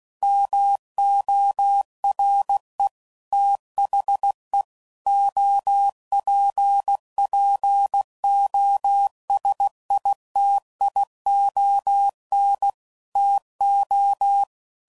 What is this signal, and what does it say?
Morse code, heard as a keyed tone, about 16 wpm, 800 Hz. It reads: MORE THE OPPOSITION TO
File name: CW Code Groups @15WPM / Technical Subjects / XL